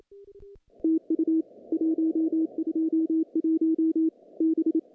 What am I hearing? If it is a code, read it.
Morse code, heard as a keyed tone, about 28 wpm, 320 Hz. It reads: TU 121 B
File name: contests / CW-SS-k5tr- -0000